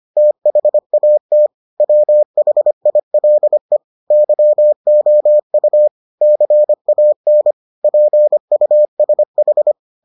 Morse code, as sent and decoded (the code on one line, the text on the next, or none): - .... .- - / .-- .... .. .-.. / -.-- --- ..- / -.-. .- -. / .--. ..- ... ....
THAT WHILE YOU CAN PUSH